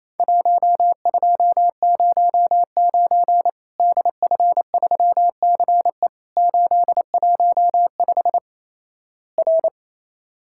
Morse code, heard as a keyed tone, about 28 wpm, 700 Hz. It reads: 1209 DF3CE 815 R
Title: Worked All Europe DX Contest - QTC Exchange - 28wpm